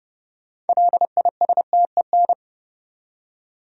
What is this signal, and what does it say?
Morse code, heard as a keyed tone, about 40 wpm, 700 Hz. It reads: LISTEN